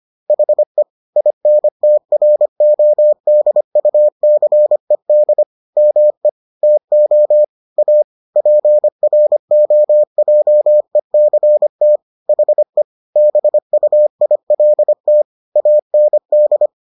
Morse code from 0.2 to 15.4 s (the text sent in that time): HE INTRODUCED ME TO A PROJECT HE BUILT